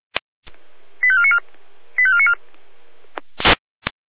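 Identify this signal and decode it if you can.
Five-tone selective call (ZVEI), heard as quick runs of five tones, tones of about 70 ms. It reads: 85494 85494
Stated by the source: ZVEI five-tone selective call 85495, received by SDR.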